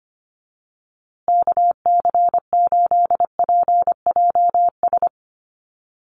Morse code, heard as a keyed tone, about 25 wpm, 700 Hz. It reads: KC8PJS
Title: USA Call Signs Repeated 3x 25wpm